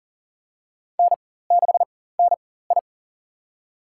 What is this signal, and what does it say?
Morse code, heard as a keyed tone, about 40 wpm, spaced as for 19 wpm, 700 Hz. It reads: N6NI
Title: USA Call Signs - 4x Character Spacing - 40wpm